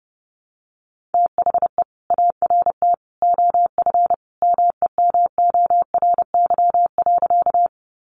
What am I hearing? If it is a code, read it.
Morse code, heard as a keyed tone, about 28 wpm, 700 Hz. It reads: THE ART OF MEMORY.
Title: Sentences from Top 1300 Words 30wpm